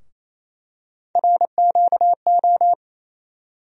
Morse code, heard as a keyed tone, about 27 wpm, 700 Hz. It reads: RQO
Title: USA Call Sign Postfixes - 3 Letters 28wpm